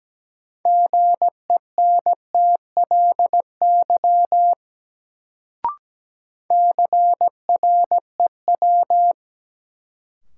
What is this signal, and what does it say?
Morse code, heard as a keyed tone, about 17 wpm, 700 Hz. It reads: GENTLY CREW
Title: Sentences from Top 1500 Words - New Word Drill Alt - 17wpm